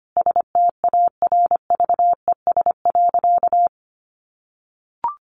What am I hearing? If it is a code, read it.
Morse code, heard as a keyed tone, about 25 wpm, 700 Hz. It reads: STARVES.
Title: A Sound Of Thunder 25wpm